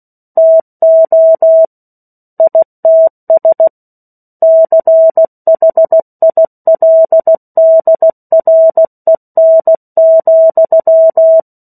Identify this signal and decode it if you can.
Morse code, heard as a keyed tone, about 16 wpm, 650 Hz. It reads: TO ITS CHILDREN,
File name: 03-Chapter-03 16 WPM